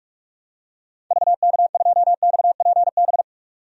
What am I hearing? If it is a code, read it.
Morse code, heard as a keyed tone, about 44 wpm, 690 Hz. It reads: VK2XPB
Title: DXCC-100 Call Signs - 6 Characters - 45wpm